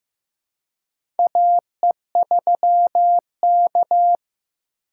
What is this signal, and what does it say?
Morse code, heard as a keyed tone, about 15 wpm, 700 Hz. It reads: AE3K
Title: USA Call Signs - 4 Characters - 2x Character Spacing - 15wpm